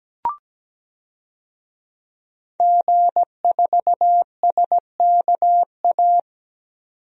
Morse code, heard as a keyed tone, about 17 wpm, 700 Hz. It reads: G4SKA
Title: DXCC-100 Call Signs - 5 Characters - 17wpm